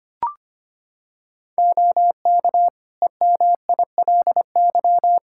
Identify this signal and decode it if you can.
Morse code, heard as a keyed tone, about 25 wpm, 700 Hz. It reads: OK EMILY
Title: Practice QSO - Exchange 3 - More Info - 25wpm